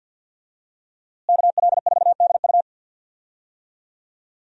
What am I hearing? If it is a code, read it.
Morse code, heard as a keyed tone, about 45 wpm, 700 Hz. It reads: KL4DU
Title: USA Call Signs - 5 Characters - Repeated 3x - 50wpm